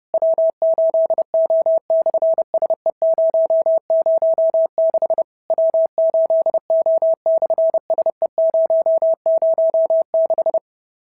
Morse code, heard as a keyed tone, about 30 wpm, 650 Hz. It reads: W8O/SE006 W8O/SE006
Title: SOTA QSO Practice 30wpm